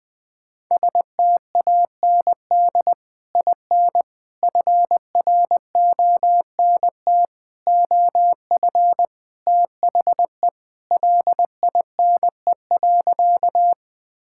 Morse code, heard as a keyed tone, about 20 wpm, 700 Hz. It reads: STAND IN FRONT OF THE LINE.
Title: Sentences with 3-Word Phrases - 20wpm